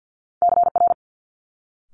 Morse code, heard as a keyed tone, about 50 wpm, 700 Hz. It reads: CR
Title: Multi-letter Phonemes Quick Fire 50wpm